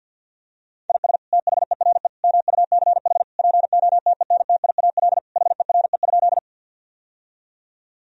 Morse code, heard as a keyed tone, about 52 wpm, 700 Hz. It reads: IS THERE MUCH POTENTIAL HERE?
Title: Sentences from Top 1000 Words - Review Part 4 - 50wpm